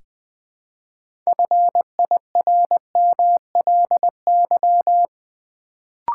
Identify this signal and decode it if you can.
Morse code, heard as a keyed tone, about 21 wpm, 700 Hz. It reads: FIRMLY
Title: Build A Word - 2 Letters At A Time - 6-Letter Words - 20wpm